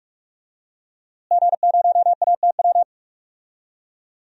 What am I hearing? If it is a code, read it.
Morse code, heard as a keyed tone, about 46 wpm, 700 Hz. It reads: G0ATW
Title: DXCC-100 Call Signs - 5 Characters - Repeated 2x - 45wpm